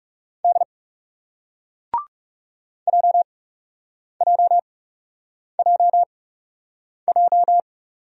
Morse code, heard as a keyed tone, about 40 wpm, 700 Hz. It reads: D J J J J